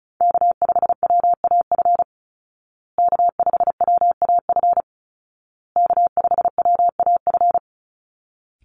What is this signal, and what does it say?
Morse code, heard as a keyed tone, about 34 wpm, 700 Hz. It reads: K5WAF K5WAF K5WAF
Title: Call Signs Repeated 2x - 35wpm